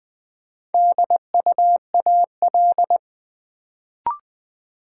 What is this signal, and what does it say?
Morse code, heard as a keyed tone, about 20 wpm, 700 Hz. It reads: DUAL
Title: Build A Word - 1 Letter At A Time - 4-Letter Words - 20wpm